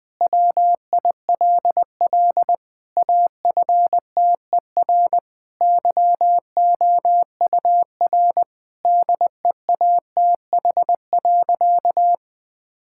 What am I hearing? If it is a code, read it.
Morse code, heard as a keyed tone, about 20 wpm, 700 Hz. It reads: WILL AFTER YOUR DEATH.
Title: Sentences from Top 1600 Words 20wpm